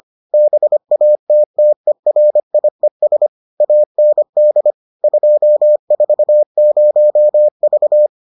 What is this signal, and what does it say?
Morse code, heard as a keyed tone, about 25 wpm, 600 Hz. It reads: BATTERIES AND 240V